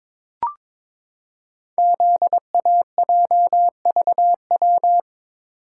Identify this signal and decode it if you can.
Morse code, heard as a keyed tone, about 23 wpm, 700 Hz. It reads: ZAJVW
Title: Code Groups - 5 Letters - 22wpm